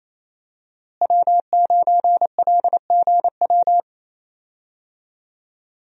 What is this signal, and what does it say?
Morse code, heard as a keyed tone, about 28 wpm, 700 Hz. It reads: W9LGW
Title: USA Call Signs - 5 Characters - Repeated 2x - 28wpm